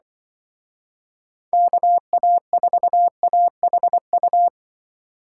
Morse code, heard as a keyed tone, about 24 wpm, 700 Hz. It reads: KA4AHU
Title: Speed Racing - USA 2x3 Call Signs 20wpm